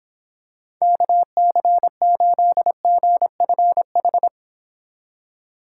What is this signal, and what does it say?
Morse code, heard as a keyed tone, about 25 wpm, 700 Hz. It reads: KC8GFH